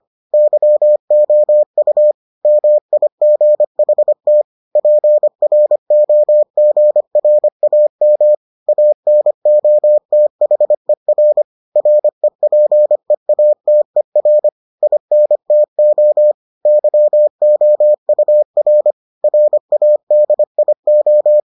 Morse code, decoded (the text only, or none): YOU MIGHT PROGRAM ANOTHER REPEATER INTO YOUR RADIO